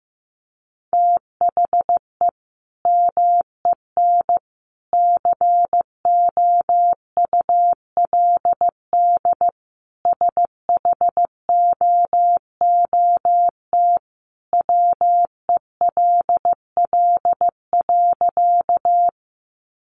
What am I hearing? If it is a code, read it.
Morse code, heard as a keyed tone, about 15 wpm, 700 Hz. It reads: THE MEN COULD SHOOT WELL.
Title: Sentences from Top 1200 Words 15wpm